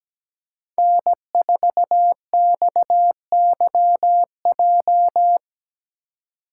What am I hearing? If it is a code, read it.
Morse code, heard as a keyed tone, about 17 wpm, 700 Hz. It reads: N4XYJ
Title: USA Call Signs - 5 Characters - 2x Character Spacing - 17wpm